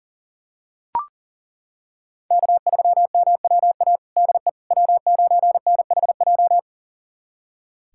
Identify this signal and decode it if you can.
Morse code, heard as a keyed tone, about 40 wpm, 700 Hz. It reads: K3MWA DE W9NHJ
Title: USA Callsign DE Callsign - 5 Characters - 40wpm